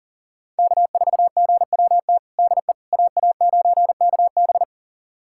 Morse code, heard as a keyed tone, about 37 wpm, 700 Hz. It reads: K4GWT DE AA9KB
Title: USA Callsign DE Callsign 40wpm